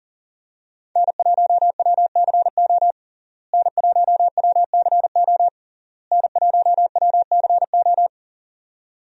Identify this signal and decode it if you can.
Morse code, heard as a keyed tone, about 40 wpm, 700 Hz. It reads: N1WCO N1WCO N1WCO
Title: USA Call Signs - 5 Characters - Repeated 3x - 40wpm